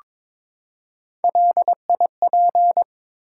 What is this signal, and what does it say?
Morse code, heard as a keyed tone, about 22 wpm, 700 Hz. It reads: LIP
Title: Three Letter Words - 22wpm